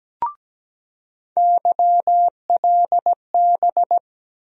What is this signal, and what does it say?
Morse code, heard as a keyed tone, about 17 wpm, 700 Hz. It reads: YLB
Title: USA Call Sign Postfixes - 3 Letters 17wpm